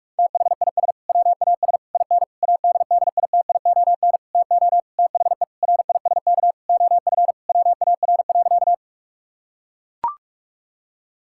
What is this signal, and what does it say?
Morse code, heard as a keyed tone, about 41 wpm, 700 Hz. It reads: THIS WAS IN ADDITION TO THE RISK OF WAR.
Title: Sentences from Top 1000 Words - Review Part 6 - 45wpm